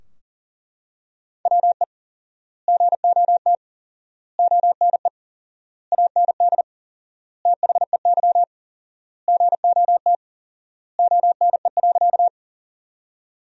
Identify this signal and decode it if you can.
Morse code, heard as a keyed tone, about 40 wpm, 700 Hz. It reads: WE GOT ONE AND THEY GOT ONE.
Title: 4x Word-Spacing - Sentences from Top 300 Words - 40wpm